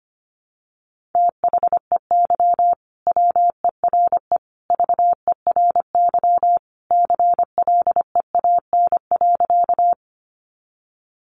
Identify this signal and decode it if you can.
Morse code, heard as a keyed tone, about 25 wpm, 700 Hz. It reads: THEY WERE VERY CLEAN.
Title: Sentences from Top 1100 Words 25wpm